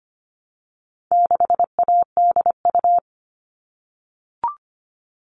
Morse code, heard as a keyed tone, about 25 wpm, 700 Hz. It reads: 6ADU
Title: All Cut Numbers - 4-Digit 25wpm